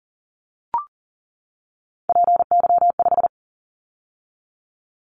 Morse code, heard as a keyed tone, about 40 wpm, 700 Hz. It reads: PY5
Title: Callsign Prefixes for Top 50 DXCC - 40wpm